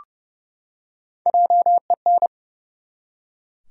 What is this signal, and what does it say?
Morse code, heard as a keyed tone, about 30 wpm, 700 Hz. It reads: JEN
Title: USA Common Names - 30wpm